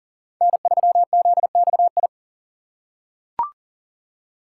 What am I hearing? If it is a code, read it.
Morse code, heard as a keyed tone, about 40 wpm, 700 Hz. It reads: N3ZXI
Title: USA Call Signs - 5 Characters - 4x Character Spacing - 40wpm